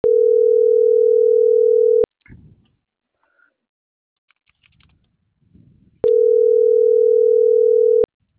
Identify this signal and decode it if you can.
Ringback tone (North America), 440 + 480 Hz.